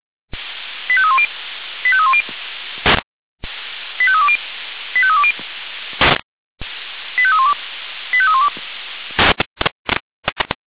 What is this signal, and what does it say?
Five-tone selective call (ZVEI), heard as quick runs of five tones, tones of about 70 ms. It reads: 85310 85310 85320 85320 85312 85312